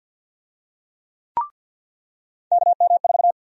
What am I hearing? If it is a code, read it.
Morse code, heard as a keyed tone, about 50 wpm, 700 Hz. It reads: KM4
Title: USA Call Sign Prefixes - 50wpm